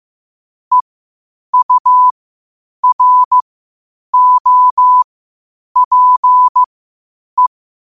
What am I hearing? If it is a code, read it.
Morse code, heard as a keyed tone, about 15 wpm, spaced as for 8 wpm, 1000 Hz. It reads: EUROPE
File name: 8wpm